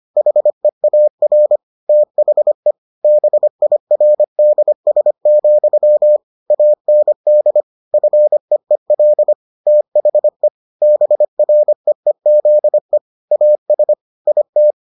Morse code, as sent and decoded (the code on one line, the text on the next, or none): .... .- .-. / - .... / -... .. .-. -.. ... --..-- / .- -. -.. / ..-. . .-.. / - .... / -... .-. . --.. / .- ... / .. -
HEAR THE BIRDS, AND FEEL THE BREEZE AS IT